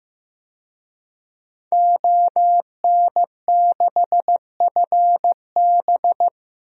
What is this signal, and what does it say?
Morse code, heard as a keyed tone, about 15 wpm, 700 Hz. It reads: ON6FB
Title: DXCC-100 Call Signs - 5 Characters - Repeated 3x - 15wpm